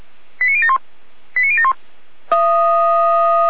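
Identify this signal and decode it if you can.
Five-tone selective call (ZVEI), heard as quick runs of five tones, tones of about 70 ms. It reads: 89071 89071
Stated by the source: SDR-received ZVEI five-tone selective call, code 89110.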